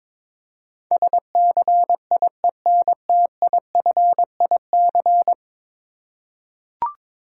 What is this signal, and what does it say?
Morse code, heard as a keyed tone, about 22 wpm, 700 Hz. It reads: SCIENTIFIC
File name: practice / 22wpm / 10 Letter Words - First 3 Letters Given - 22wpm